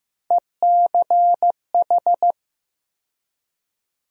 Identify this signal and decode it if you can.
Morse code, heard as a keyed tone, about 16 wpm, 700 Hz. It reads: ECH
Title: USA Call Sign Postfixes - 3 Letters 15wpm